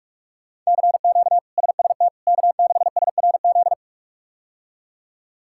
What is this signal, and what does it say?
Morse code, heard as a keyed tone, about 46 wpm, 700 Hz. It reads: CQ SST K6SRZ
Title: K1USN SST QSO Practice - 45wpm